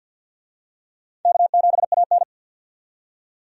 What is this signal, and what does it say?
Morse code, heard as a keyed tone, about 52 wpm, 700 Hz. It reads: K7AN
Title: USA Call Signs - 4 Characters - 50wpm